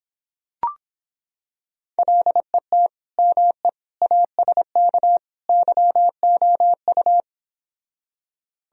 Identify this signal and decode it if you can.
Morse code, heard as a keyed tone, about 26 wpm, 700 Hz. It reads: LET ME ASK YOU